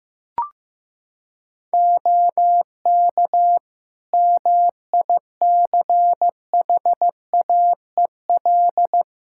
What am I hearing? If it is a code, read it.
Morse code, heard as a keyed tone, about 15 wpm, 700 Hz. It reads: OK MICHAEL
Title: K1USN SST QSO Practice 15wpm